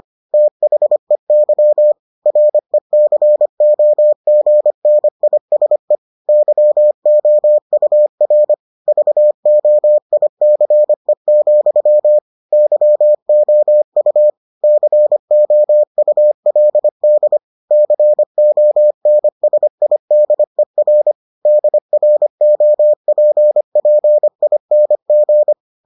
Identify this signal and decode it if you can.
Morse code, heard as a keyed tone, about 25 wpm, 600 Hz. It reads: THEY RECOGNISE YOUR VOICE, YOU COULD CONSIDER DROPPING